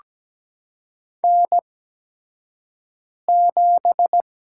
Morse code, heard as a keyed tone, about 17 wpm, spaced as for 5 wpm, 700 Hz. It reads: N7
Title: USA Call Signs - 4 Characters - 8x Character Spacing - 17wpm